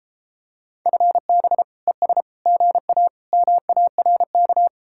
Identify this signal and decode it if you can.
Morse code, heard as a keyed tone, about 33 wpm, 700 Hz. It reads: FB ES GA MARK